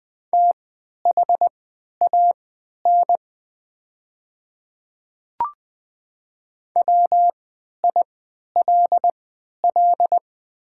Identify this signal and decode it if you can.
Morse code, heard as a keyed tone, about 20 wpm, spaced as for 11 wpm, 700 Hz. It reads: THAN WILL
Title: 3x Character Spacing - Top 100 Words - 20wpm